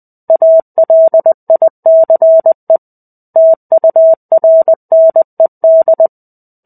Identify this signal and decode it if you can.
Morse code, heard as a keyed tone, about 20 wpm, 650 Hz. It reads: ALICE TURNED
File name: 05-Chapter-05 20 WPM